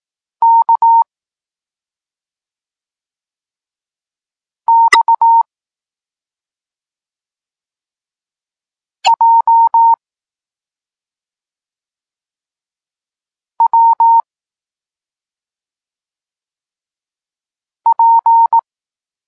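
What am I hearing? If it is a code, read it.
Morse code, heard as a keyed tone, about 18 wpm, 920 Hz. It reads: K X J W P
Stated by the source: Morse code lesson for learning the letters K and X, from the SP9MOA Niepołomicki Klub Krótkofalowców course.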